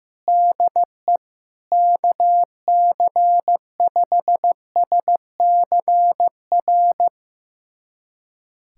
Morse code, heard as a keyed tone, about 15 wpm, 700 Hz. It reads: DE KC5SCR
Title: USA Callsign DE Callsign - 6 Characters - 15wpm